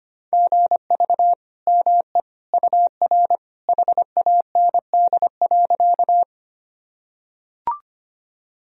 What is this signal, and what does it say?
Morse code, heard as a keyed tone, about 25 wpm, 700 Hz. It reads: GV ME UR HAND.
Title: Sentences from Top 200 Words and Abbreviations 25wpm